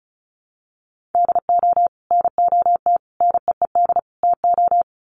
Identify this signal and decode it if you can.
Morse code, heard as a keyed tone, about 34 wpm, 700 Hz. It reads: DO NOT NEED TO
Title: Sets of 4 Words 35wpm